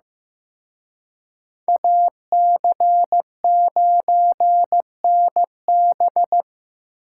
Morse code, heard as a keyed tone, about 17 wpm, 700 Hz. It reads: AC9NB
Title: Speed Racing - USA 2x2 Call Signs 15wpm